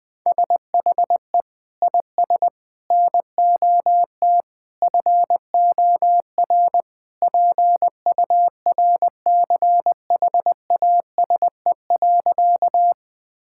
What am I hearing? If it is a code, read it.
Morse code, heard as a keyed tone, about 21 wpm, 700 Hz. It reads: SHE IS NOT FOR PURCHASE.
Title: Sentences from Top 1000 Words - Review Part 5 - 20wpm